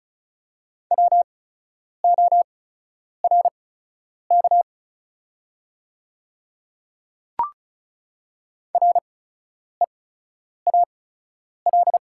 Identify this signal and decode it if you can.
Morse code, heard as a keyed tone, about 36 wpm, spaced as for 10 wpm, 690 Hz. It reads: WORK REAL
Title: 8x Character Spacing - Top 500 Words - 35wpm